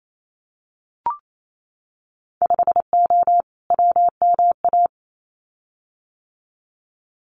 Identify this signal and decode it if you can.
Morse code, heard as a keyed tone, about 28 wpm, 690 Hz. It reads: 5O WMA
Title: Winter Field Day Exchanges Encore 28wpm